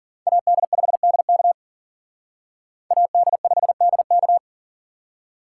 Morse code, heard as a keyed tone, about 48 wpm, 700 Hz. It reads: AD5DK AD5DK
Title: Speed Racing - USA 2x2 Call Signs 40wpm